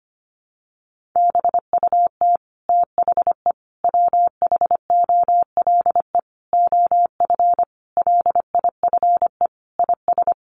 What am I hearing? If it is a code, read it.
Morse code, heard as a keyed tone, about 25 wpm, 700 Hz. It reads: BUT THE WHOLE OF LIFE IS
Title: Sentences from Top 1500 Words 25wpm